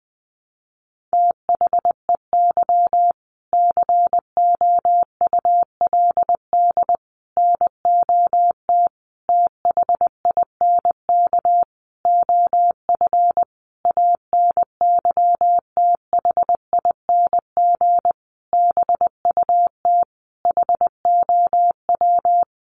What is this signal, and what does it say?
Morse code, heard as a keyed tone, about 20 wpm, 700 Hz. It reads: THEY COULD NOT THINK OF ANYTHING BUT HOW
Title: Sentences from Top 1000 Words 20wpm